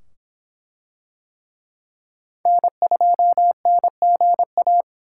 Morse code, heard as a keyed tone, about 24 wpm, 700 Hz. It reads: N2NGA